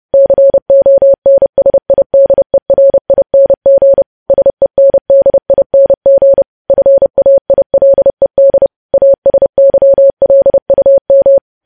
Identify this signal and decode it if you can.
Morse code, heard as a keyed tone, about 30 wpm, 550 Hz. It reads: CONSIDERING SENDING FAILED ASYLUM
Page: news headlines in morse code at 30 wpm